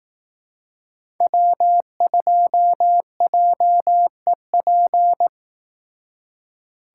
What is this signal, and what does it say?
Morse code, heard as a keyed tone, about 18 wpm, 700 Hz. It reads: W2JEP
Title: Speed Racing - USA 1x3 Call Signs 15wpm